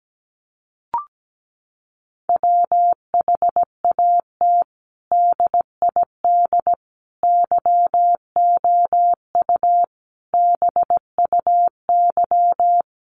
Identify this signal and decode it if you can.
Morse code, heard as a keyed tone, about 17 wpm, 700 Hz. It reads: WHAT DID YOU BUY
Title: Sentences from Top 700 Words 17wpm